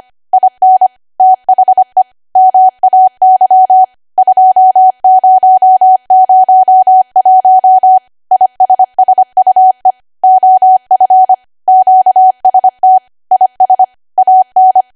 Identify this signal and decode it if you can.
Morse code, heard as a keyed tone, about 25 wpm, 750 Hz. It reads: IN THE MAY 2001 ISSUE OF QST IS AN
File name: Archive / 25 WPM / 25WPM